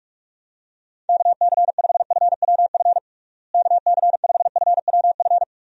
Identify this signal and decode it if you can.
Morse code, heard as a keyed tone, about 42 wpm, 700 Hz. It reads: KC5FWF KC5FWF
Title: USA Call Signs - 6 Characters - Repeated 2x - 45wpm